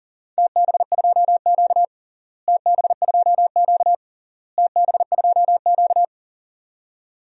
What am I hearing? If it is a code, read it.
Morse code, heard as a keyed tone, about 37 wpm, 700 Hz. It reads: TB2Q TB2Q TB2Q
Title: DXCC-100 Call Signs - 4 Characters - Repeated 3x - 40wpm